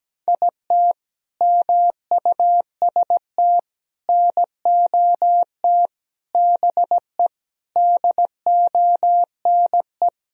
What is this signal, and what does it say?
Morse code, heard as a keyed tone, about 17 wpm, 700 Hz. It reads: IT MUST NOT BE DONE
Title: Sentences from Top 1000 Words - Review Part 1 - 17wpm